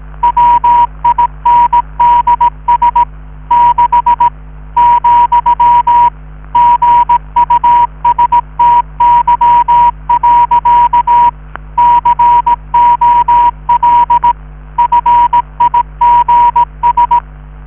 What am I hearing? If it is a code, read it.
Morse code, heard as a keyed tone, about 18 wpm, 960 Hz. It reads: WINDS 6 , GUSTY. COL FIGS